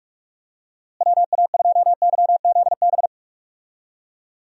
Morse code, heard as a keyed tone, about 43 wpm, 700 Hz. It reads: WA2YZB
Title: USA Call Signs - 6 Characters - Repeated 3x - 45wpm